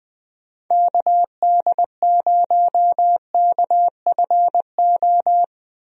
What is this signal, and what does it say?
Morse code, heard as a keyed tone, about 20 wpm, 700 Hz. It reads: KD0KFO
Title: USA Call Signs - 6 Characters - Repeated 3x - 20wpm